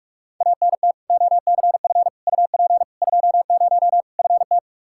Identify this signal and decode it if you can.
Morse code, heard as a keyed tone, about 41 wpm, 700 Hz. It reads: ANT OCF UP 20 FT
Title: K1USN SST QSO Practice 45wpm